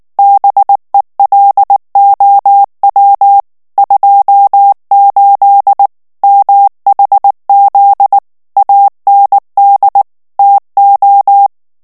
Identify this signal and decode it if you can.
Morse code, heard as a keyed tone, about 19 wpm, 800 Hz. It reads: BELOW 28 MHZ AND TO
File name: ANGLAIS texte clair 3 vitesse 18 mots minute